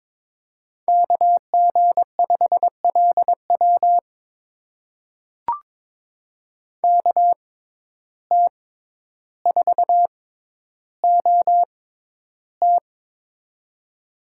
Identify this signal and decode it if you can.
Morse code, heard as a keyed tone, about 22 wpm, spaced as for 11 wpm, 700 Hz. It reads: KG5LW KT4OT
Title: USA Call Signs - 5 Characters - 6x Character Spacing - 22wpm